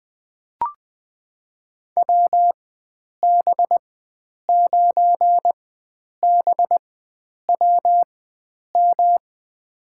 Morse code, heard as a keyed tone, about 21 wpm, spaced as for 9 wpm, 700 Hz. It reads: WB9BWM
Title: USA Call Signs - 4x Character Spacing - 20wpm